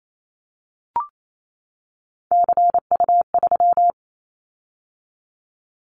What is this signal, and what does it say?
Morse code, heard as a keyed tone, about 28 wpm, 700 Hz. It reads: CU3
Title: Callsign Prefixes for Top 100 DXCC - 28wpm